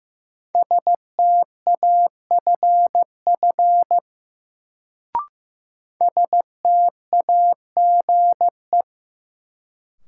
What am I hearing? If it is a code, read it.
Morse code, heard as a keyed tone, about 15 wpm, 700 Hz. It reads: STAFF STAGE
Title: Sentences from Top 1200 Words - New Word Drill - 15wpm